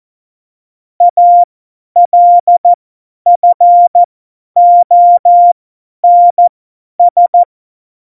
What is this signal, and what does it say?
Morse code, heard as a keyed tone, about 14 wpm, spaced as for 10 wpm, 700 Hz. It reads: ALFONS
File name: Morsekurs / Witze / Witz 26. Februar 2019, 50 F70